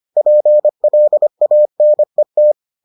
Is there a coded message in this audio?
Yes: Morse code from 0.2 to 2.5 s, heard as a keyed tone, about 25 wpm, 600 Hz. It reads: PLANET